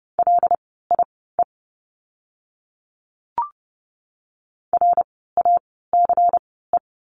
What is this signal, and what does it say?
Morse code, heard as a keyed tone, about 30 wpm, spaced as for 17 wpm, 700 Hz. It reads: LIE RACE
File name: Lesson 16f - Letter c - Words 30wpm